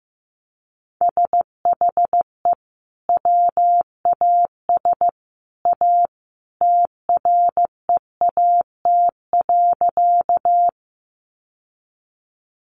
Morse code, heard as a keyed tone, about 15 wpm, 700 Hz. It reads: SHE WAS A TREAT.
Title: Sentences from Top 1300 Words 15wpm